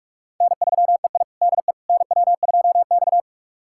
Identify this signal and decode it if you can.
Morse code, heard as a keyed tone, about 46 wpm, 700 Hz. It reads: N3EI DE NW2X
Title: USA Callsign DE Callsign - 4 Characters - 45wpm